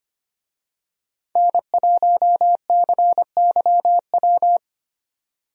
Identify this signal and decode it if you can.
Morse code, heard as a keyed tone, about 25 wpm, 700 Hz. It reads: N1CYW
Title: USA Call Signs - 5 Characters - 25wpm